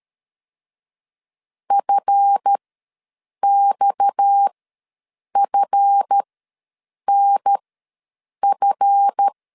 Morse code, heard as a keyed tone, about 13 wpm, spaced as for 7 wpm, 800 Hz. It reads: FXFNF
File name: LEZIONI / 7 GF9Z / 7 GF ws 12wpm 3140